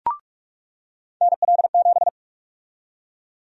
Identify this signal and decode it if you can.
Morse code, heard as a keyed tone, about 46 wpm, 700 Hz. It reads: NL7
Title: USA Call Sign Prefixes - 45wpm